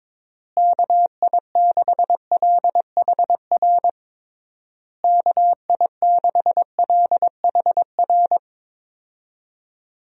Morse code, heard as a keyed tone, about 22 wpm, 700 Hz. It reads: KI6LHR KI6LHR
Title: USA Call Signs - 6 Characters - Repeated 3x - 22wpm